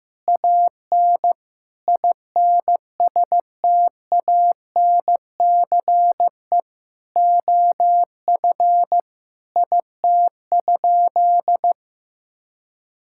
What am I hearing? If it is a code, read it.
Morse code, heard as a keyed tone, about 15 wpm, 700 Hz. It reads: AN INSTANCE OF IT?
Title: Sentences from Top 1000 Words - Review Part 5 - 15wpm